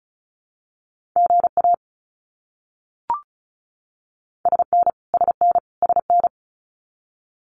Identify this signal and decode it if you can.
Morse code, heard as a keyed tone, about 34 wpm, 700 Hz. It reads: GA SN SN SN
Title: Abbreviations Repeated 3x - 35wpm